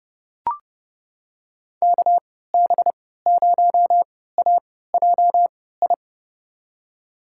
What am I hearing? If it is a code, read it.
Morse code, heard as a keyed tone, about 31 wpm, spaced as for 17 wpm, 700 Hz. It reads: KB0AJI
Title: USA Call Signs - 6 Characters - 3x Character Spacing - 30wpm